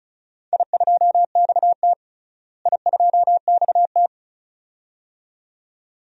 Morse code, heard as a keyed tone, about 34 wpm, 700 Hz. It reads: I2XT I2XT
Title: DXCC-100 Call Signs - 4 Characters - Repeated 2x - 35wpm